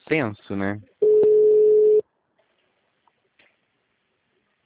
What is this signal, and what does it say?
Ringback tone (Europe, 425 Hz), with speech.